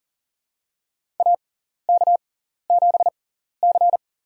Morse code, heard as a keyed tone, about 40 wpm, spaced as for 14 wpm, 700 Hz. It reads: AK7C
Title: USA Call Signs - 6x Character Spacing - 40wpm